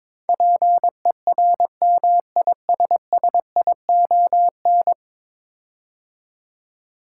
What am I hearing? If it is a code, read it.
Morse code, heard as a keyed tone, about 22 wpm, 700 Hz. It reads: PERMISSION
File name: practice / 22wpm / 10 Letter Words - First 3 Letters Given - 22wpm